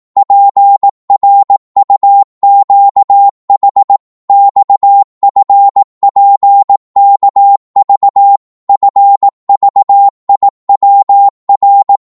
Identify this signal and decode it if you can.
Morse code, heard as a keyed tone, about 18 wpm, 800 Hz. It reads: PRUQH XFPKV FVIWR
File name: Morse / ltr-18wpm